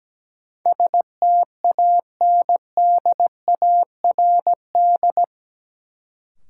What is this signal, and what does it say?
Morse code, heard as a keyed tone, about 17 wpm, 700 Hz. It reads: STANDARD
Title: Sentences from Top 1200 Words - New Word Drill Alt - 17wpm